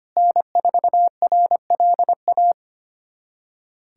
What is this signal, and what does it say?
Morse code, heard as a keyed tone, about 25 wpm, 700 Hz. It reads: N4RLA
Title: USA Callsign DE Callsign - 5 Characters - 25wpm